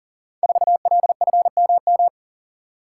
Morse code, heard as a keyed tone, about 41 wpm, 700 Hz. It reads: 4LFMM
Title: Code Groups - 5 Letter:Numbers - 40wpm